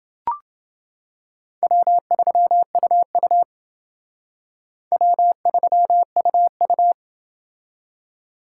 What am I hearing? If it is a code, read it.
Morse code, heard as a keyed tone, about 30 wpm, 700 Hz. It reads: W3UU W3UU